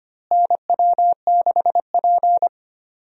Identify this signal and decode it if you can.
Morse code, heard as a keyed tone, about 25 wpm, 700 Hz. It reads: NW6P
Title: USA Call Signs - 4 Characters - Repeated 3x - 25wpm